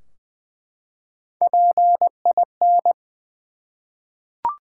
Morse code, heard as a keyed tone, about 20 wpm, 700 Hz. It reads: PIN
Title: Three Letter Words - 20wpm